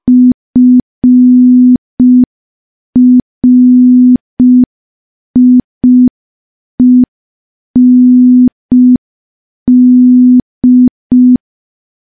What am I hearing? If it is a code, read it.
Morse code, heard as a keyed tone, about 5 wpm, 250 Hz. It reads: FRIEND
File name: morse / morse-training-05WPM